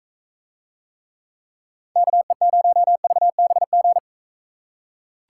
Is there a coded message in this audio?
Yes: Morse code from 2.0 to 4.0 s, heard as a keyed tone, about 40 wpm, 700 Hz. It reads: KE0VBG